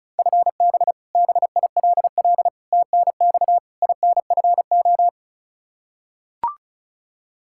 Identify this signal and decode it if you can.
Morse code, heard as a keyed tone, about 34 wpm, 700 Hz. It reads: FB BILL TNX INFO
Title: Practice QSO - Exchange 3 - More Info - 35wpm